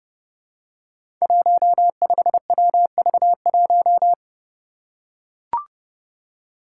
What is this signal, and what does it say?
Morse code, heard as a keyed tone, about 30 wpm, 700 Hz. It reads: 15WV1
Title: Five Challenging Letter-Number Combinations 30wpm